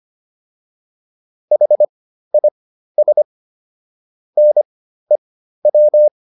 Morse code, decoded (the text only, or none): HIS NEW